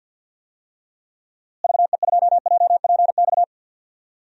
Morse code, heard as a keyed tone, about 52 wpm, 700 Hz. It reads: VE2JPX
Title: DXCC-100 Call Signs - 6 Characters - Repeated 2x - 50wpm